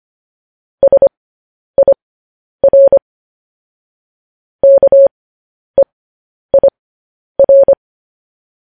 Morse code, heard as a keyed tone, about 25 wpm, spaced as for 10 wpm, 550 Hz. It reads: SIR KEIR
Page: news headlines in morse code at 10 wpm